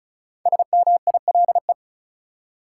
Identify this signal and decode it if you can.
Morse code, heard as a keyed tone, about 36 wpm, 700 Hz. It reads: SMILE
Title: Five Letter Words - 35wpm